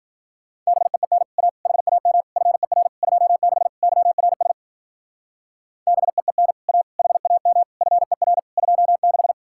Morse code, heard as a keyed tone, about 51 wpm, 700 Hz. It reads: BEEN A HAM FER 26 YRS BEEN A HAM FER 26